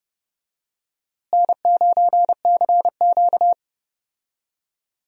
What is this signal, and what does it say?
Morse code, heard as a keyed tone, about 30 wpm, 700 Hz. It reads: N9CQ
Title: USA Call Signs - 4 Characters - Repeated 2x - 30wpm